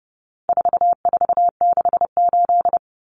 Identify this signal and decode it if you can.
Morse code, heard as a keyed tone, about 28 wpm, 700 Hz. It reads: 4468
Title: Numbers - 4-Digit 30wpm